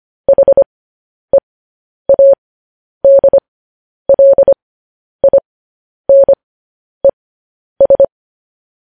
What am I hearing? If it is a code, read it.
Morse code, heard as a keyed tone, about 25 wpm, spaced as for 10 wpm, 550 Hz. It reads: HEADLINES
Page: news headlines in morse code at 10 wpm